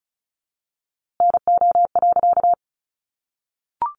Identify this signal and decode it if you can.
Morse code, heard as a keyed tone, about 34 wpm, 700 Hz. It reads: NO.
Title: A Sound Of Thunder 35wpm